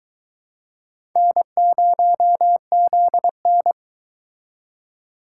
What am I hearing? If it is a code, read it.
Morse code, heard as a keyed tone, about 22 wpm, 700 Hz. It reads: N0ZN